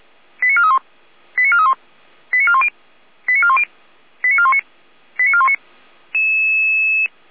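Five-tone selective call (ZVEI), heard as quick runs of five tones, tones of about 70 ms. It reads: 89431 89431 89410 89410 89419 89419
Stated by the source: ZVEI five-tone selective call received by SDR.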